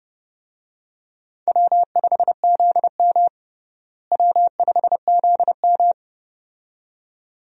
Morse code, heard as a keyed tone, about 30 wpm, 700 Hz. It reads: W5ZM W5ZM
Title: USA Call Signs - 4 Characters - Repeated 2x - 30wpm